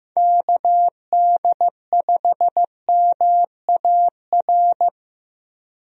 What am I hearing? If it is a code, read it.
Morse code, heard as a keyed tone, about 16 wpm, 700 Hz. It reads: KD5MAR
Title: USA Call Signs - 6 Characters - Repeated 3x - 15wpm